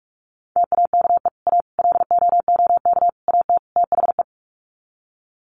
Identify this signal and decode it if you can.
Morse code, heard as a keyed tone, about 44 wpm, 700 Hz. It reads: TAKE A LOOK AT THE
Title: Sets of 5 Words 45wpm